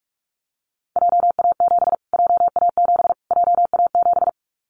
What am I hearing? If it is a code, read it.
Morse code, heard as a keyed tone, about 44 wpm, 700 Hz. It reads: JA7 JA7 JA7
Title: Callsign Prefixes for Top 25 DXCC - Repeated 3x - 45wpm